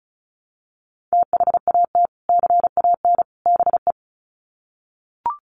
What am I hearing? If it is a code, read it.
Morse code, heard as a keyed tone, about 34 wpm, 700 Hz. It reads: THAT CAN BE
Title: Sets of 3 Words 35wpm